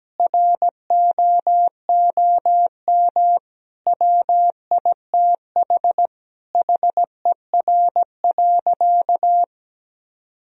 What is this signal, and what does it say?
Morse code, heard as a keyed tone, about 17 wpm, 700 Hz. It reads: ROOM WITH HER.
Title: Sentences from Top 1000 Words - Review Part 2 - 17wpm